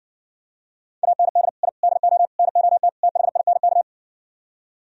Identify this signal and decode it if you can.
Morse code, heard as a keyed tone, about 58 wpm, 700 Hz. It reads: AND I DO NOT THINK